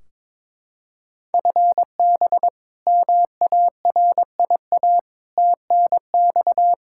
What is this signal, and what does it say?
Morse code, heard as a keyed tone, about 22 wpm, 700 Hz. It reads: FB MARIA TNX
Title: K1USN SST QSO Practice 22wpm